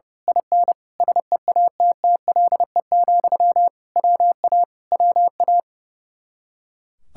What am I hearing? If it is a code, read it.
Morse code, heard as a keyed tone, about 28 wpm, 700 Hz. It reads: IN SEATTLE, WA WA
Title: Speed-Racing Common QSO Phrases Encore - 30wpm